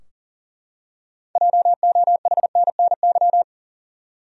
Morse code, heard as a keyed tone, about 40 wpm, 700 Hz. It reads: JOHNNY
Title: USA Common Names - 40wpm